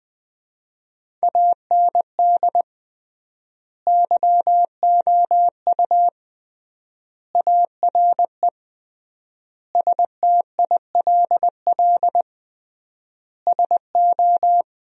Morse code, heard as a keyed tone, about 20 wpm, 700 Hz. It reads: AND YOU ARE STILL SO
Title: 3x Word-Spacing - Sentences from Top 400 Words - 20wpm